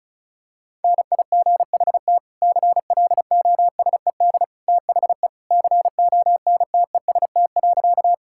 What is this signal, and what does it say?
Morse code, heard as a keyed tone, about 36 wpm, 700 Hz. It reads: NIGHT CLOSED THE CONTEST.
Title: Sentences from Top 1600 Words 35wpm